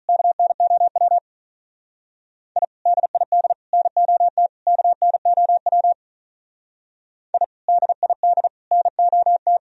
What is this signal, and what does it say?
Morse code, heard as a keyed tone, about 44 wpm, 700 Hz. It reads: KNOW I DID NOT KNOW I DID NOT
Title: Speed-Racing - Sets of 4 Words 35wpm